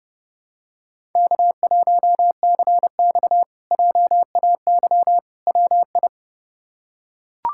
Morse code, heard as a keyed tone, about 31 wpm, 700 Hz. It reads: K1CX JAY WI
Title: K1USN SST QSO Practice - 30wpm